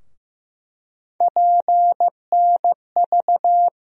Morse code, heard as a keyed tone, about 15 wpm, 700 Hz. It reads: PNV
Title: USA Call Sign Postfixes - 3 Letters 15wpm